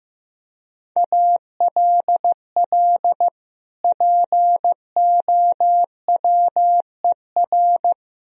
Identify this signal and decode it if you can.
Morse code, heard as a keyed tone, about 15 wpm, 700 Hz. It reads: ALL POWER
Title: Sentences from Top 1000 Words - Review Part 2 - 15wpm